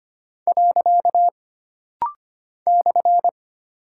Morse code, heard as a keyed tone, about 25 wpm, 700 Hz. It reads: . /
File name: Single Punctuation-Prosign - Rapid-Fire - 25wpm